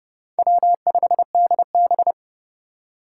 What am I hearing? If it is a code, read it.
Morse code, heard as a keyed tone, about 30 wpm, 700 Hz. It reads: W5DB
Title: USA Call Signs - 4 Characters - 4x Character Spacing - 30wpm